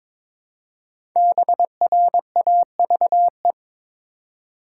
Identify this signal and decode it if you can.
Morse code, heard as a keyed tone, about 22 wpm, 700 Hz. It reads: BRAVE
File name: 5 Letter Words - First 3 Letters Given - 22wpm